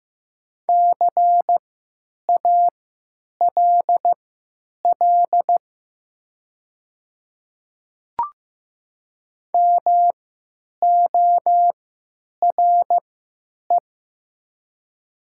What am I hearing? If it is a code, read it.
Morse code, heard as a keyed tone, about 15 wpm, spaced as for 9 wpm, 700 Hz. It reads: CALL MORE
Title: 3x Character Spacing - Top 200 Words - 15wpm